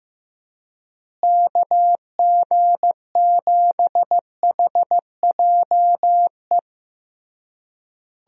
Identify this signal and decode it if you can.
Morse code, heard as a keyed tone, about 15 wpm, 700 Hz. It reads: KG7HJE
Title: USA Call Signs - 6 Characters - Repeated 3x - 15wpm